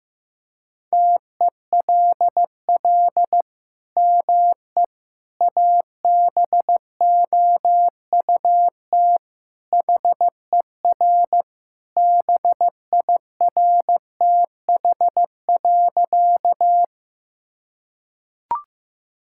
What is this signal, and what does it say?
Morse code, heard as a keyed tone, about 15 wpm, 700 Hz. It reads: TELL ME ABOUT HER BIRTH.
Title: Sentences from Top 1000 Words - Review Part 5 - 15wpm